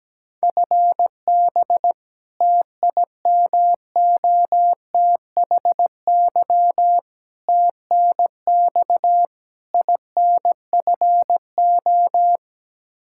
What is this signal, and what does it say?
Morse code, heard as a keyed tone, about 17 wpm, 700 Hz. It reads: FB TIMOTHY TNX INFO
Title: Practice QSO - Exchange 3 - More Info - 17wpm